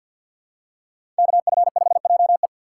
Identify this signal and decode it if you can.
Morse code, heard as a keyed tone, about 53 wpm, 700 Hz. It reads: KF5JE
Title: USA Call Signs - 5 Characters - 3x Character Spacing - 50wpm